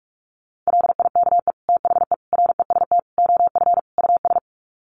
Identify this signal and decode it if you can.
Morse code, heard as a keyed tone, about 44 wpm, 700 Hz. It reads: LIKE THE REST OF US
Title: Sets of 5 Words Encore2 45wpm